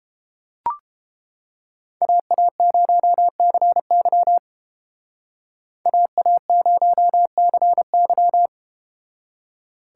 Morse code, heard as a keyed tone, about 32 wpm, 700 Hz. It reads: AA0CY AA0CY